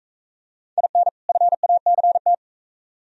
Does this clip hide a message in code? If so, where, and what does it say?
Yes: Morse code from 0.8 to 2.3 s, heard as a keyed tone, about 40 wpm, 700 Hz. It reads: IN FACT